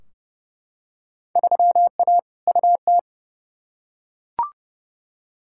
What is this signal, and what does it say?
Morse code, heard as a keyed tone, about 30 wpm, 700 Hz. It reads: 3A UT
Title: Field Day Exchanges - 30wpm